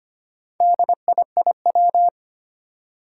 Morse code, heard as a keyed tone, about 25 wpm, 700 Hz. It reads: DIIW
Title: Four Challenging Letter-Number Combinations 25wpm